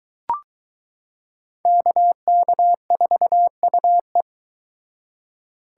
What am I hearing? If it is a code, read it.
Morse code, heard as a keyed tone, about 22 wpm, 700 Hz. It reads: KK4UE